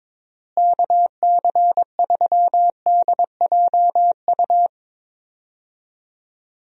Morse code, heard as a keyed tone, about 22 wpm, 700 Hz. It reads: KC3DJU
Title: USA Call Signs - 6 Characters - Repeated 3x - 22wpm